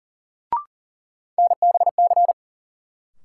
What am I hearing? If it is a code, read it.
Morse code, heard as a keyed tone, about 40 wpm, 700 Hz. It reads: NBC
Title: Sets of 5 Words Encore - New Word Drill - 40wpm